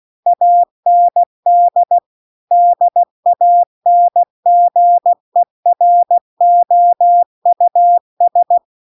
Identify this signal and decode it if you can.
Morse code, heard as a keyed tone, about 16 wpm, 700 Hz. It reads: AND DANGEROUS
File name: CW-The-Big-Twoheaded-River-2-16WPM